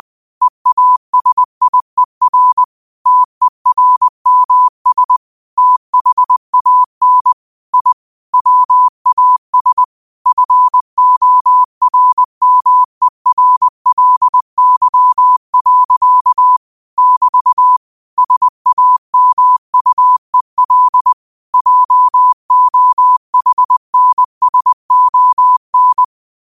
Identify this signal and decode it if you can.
Morse code, heard as a keyed tone, about 20 wpm, 1000 Hz. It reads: EASIER TERMS THAN I WAS FORMERLY. <BT> SAMUEL JOHNSON